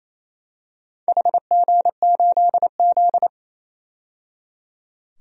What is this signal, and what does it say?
Morse code, heard as a keyed tone, about 28 wpm, 700 Hz. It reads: HG8Z
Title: DXCC-100 Call Signs - 4 Characters - 28wpm